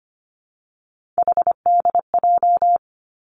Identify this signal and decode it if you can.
Morse code, heard as a keyed tone, about 25 wpm, 700 Hz. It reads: HDJ
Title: Three Challenging Letter-Number Combinations 25wpm